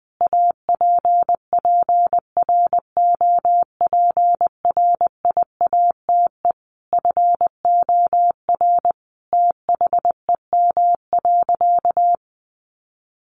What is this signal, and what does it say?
Morse code, heard as a keyed tone, about 20 wpm, 700 Hz. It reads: APPROPRIATE FOR THEM.
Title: Sentences from Top 900 Words 20wpm